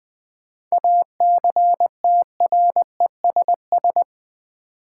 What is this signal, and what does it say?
Morse code, heard as a keyed tone, about 20 wpm, 700 Hz. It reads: ACTRESS
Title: 7 Letter Words - First 3 Letters Given - 20wpm